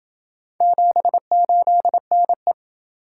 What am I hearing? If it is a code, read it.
Morse code, heard as a keyed tone, about 27 wpm, 700 Hz. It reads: 78NE